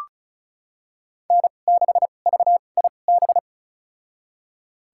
Morse code, heard as a keyed tone, about 36 wpm, spaced as for 25 wpm, 700 Hz. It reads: N6VIB
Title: USA Call Signs - 5 Characters - 2x Character Spacing - 35wpm